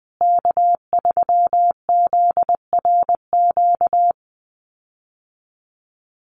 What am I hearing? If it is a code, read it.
Morse code, heard as a keyed tone, about 20 wpm, 700 Hz. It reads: K3ZRQ
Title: USA Call Signs Repeated 3x 20wpm